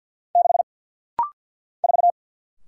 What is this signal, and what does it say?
Morse code, heard as a keyed tone, about 49 wpm, 700 Hz. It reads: 6 4
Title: Single Letter-Number - Rapid-Fire 50wpm